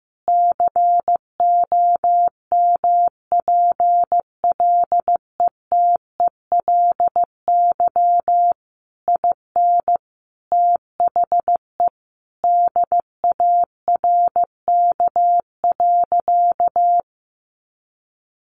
Morse code, heard as a keyed tone, about 15 wpm, 700 Hz. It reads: COMPLETELY IN THE DARK.
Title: Sentences from Top 1100 Words 15wpm